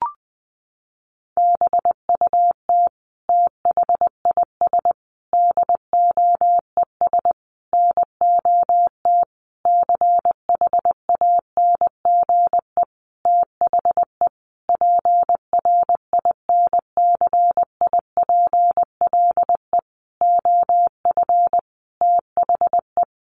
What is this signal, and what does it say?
Morse code, heard as a keyed tone, about 20 wpm, 700 Hz. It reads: BUT THIS DOES NOT CHANGE THE PRINCIPLE OF THE
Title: Sentences from Top 1400 Words 20wpm